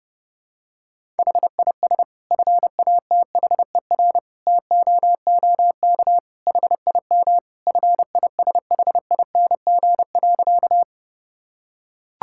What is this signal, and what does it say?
Morse code, heard as a keyed tone, about 30 wpm, 700 Hz. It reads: HIS FATHER TOOK HIM FISHING.
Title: Sentences from Top 1000 Words - Review Part 5 - 30wpm